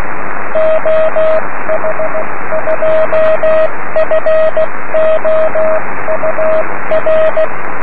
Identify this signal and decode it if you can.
Morse code, heard as a keyed tone, about 16 wpm, 630 Hz, with radio noise. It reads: OH2FOUR
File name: om3pv 70mhz / zaznam / oh2four